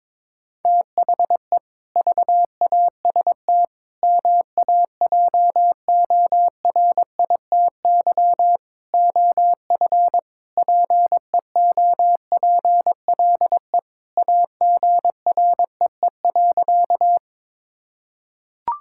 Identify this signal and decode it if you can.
Morse code, heard as a keyed tone, about 22 wpm, 700 Hz. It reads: THE VAST MAJORITY OF PEOPLE AGREE.
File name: practice / 22wpm / Sentences with 4-Word Phrases - 22wpm